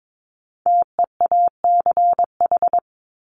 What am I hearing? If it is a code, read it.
Morse code, heard as a keyed tone, about 22 wpm, 700 Hz. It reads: TEACH
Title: Top 500 Words Repeated 22wpm